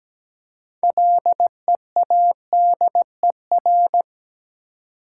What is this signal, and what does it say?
Morse code, heard as a keyed tone, about 17 wpm, 700 Hz. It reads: LEADER
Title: Six Letter Words - 17wpm